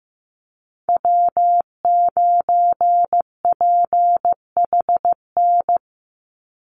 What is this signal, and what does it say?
Morse code, heard as a keyed tone, about 15 wpm, 700 Hz. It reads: W9PHN
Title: USA Call Signs 15wpm